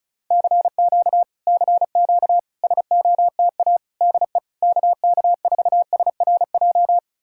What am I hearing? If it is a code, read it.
Morse code, heard as a keyed tone, about 41 wpm, 700 Hz. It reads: CQ CQ SOTA DE KK4SRJ